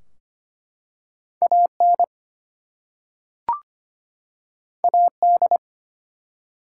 Morse code, heard as a keyed tone, about 25 wpm, 700 Hz. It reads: AN AD
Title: Two Letter Words - 25wpm